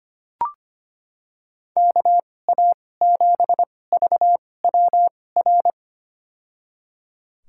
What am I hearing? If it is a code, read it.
Morse code, heard as a keyed tone, about 25 wpm, spaced as for 18 wpm, 700 Hz. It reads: KA7VWR
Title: USA Call Signs - 6 Characters - 2x Character Spacing - 25wpm